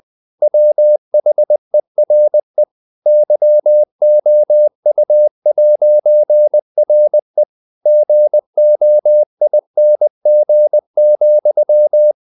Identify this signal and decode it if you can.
Morse code, heard as a keyed tone, about 20 wpm, 600 Hz. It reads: WHERE YOU'RE GOING,
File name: CW / Practice / EnglishQuotations0320